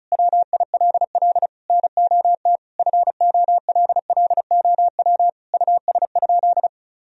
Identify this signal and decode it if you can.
Morse code, heard as a keyed tone, about 36 wpm, 700 Hz. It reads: WILL NOT FOLLOW US?
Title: Sentences from Top 1000 Words - Review Part 1 - 35wpm